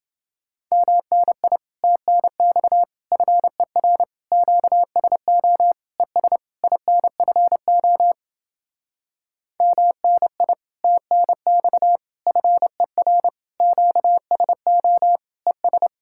Morse code, heard as a keyed tone, about 32 wpm, 700 Hz. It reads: MNI TNX FER QSO ES INFO MNI TNX FER QSO ES